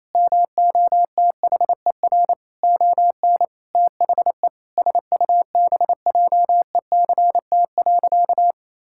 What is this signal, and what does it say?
Morse code, heard as a keyed tone, about 28 wpm, 700 Hz. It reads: MOTHER ON THE SUBJECT.
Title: Sentences from Top 1000 Words - Review Part 6 - 28wpm